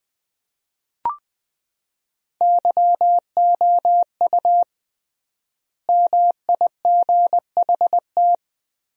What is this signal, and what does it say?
Morse code, heard as a keyed tone, about 20 wpm, 700 Hz. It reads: YOU MIGHT
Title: 3x Word-Spacing - Sentences from Top 300 Words - 20wpm